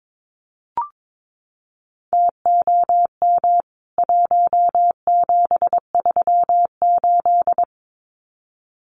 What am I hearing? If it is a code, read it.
Morse code, heard as a keyed tone, about 22 wpm, 700 Hz. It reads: TOM 1738
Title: CWT Exchange 22wpm